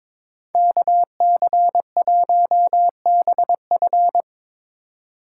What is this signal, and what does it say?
Morse code, heard as a keyed tone, about 25 wpm, 700 Hz. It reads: KC1BF
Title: Speed Racing - USA 2x2 Call Signs 22wpm